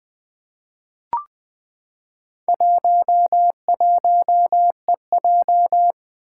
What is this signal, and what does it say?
Morse code, heard as a keyed tone, about 20 wpm, 700 Hz. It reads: 11EJ
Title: Four Challenging Letter-Number Combinations 20wpm